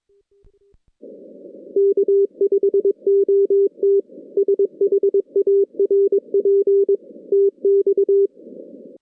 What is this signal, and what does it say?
Morse code, heard as a keyed tone, about 22 wpm, 400 Hz. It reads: K5OT SHARP TX
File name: 2016 JAN CW NAQP / 2016 Jan naqp k5ot - 0-10